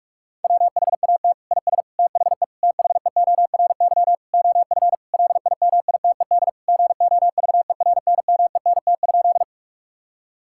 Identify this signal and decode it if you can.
Morse code, heard as a keyed tone, about 46 wpm, 700 Hz. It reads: WHAT IS THE THEORY OF LIMITED GOVERNMENT?
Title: Sentences from Top 1000 Words - Review Part 6 - 45wpm